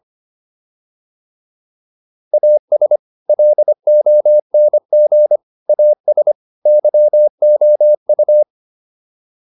Morse code, heard as a keyed tone, about 25 wpm, 600 Hz. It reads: AS LONG AS YOU
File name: WD304 25